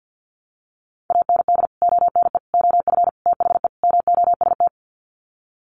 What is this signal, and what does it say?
Morse code, heard as a keyed tone, about 50 wpm, 700 Hz. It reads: AND ONE OF THE MOST
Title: Sets of 5 Words Encore 50wpm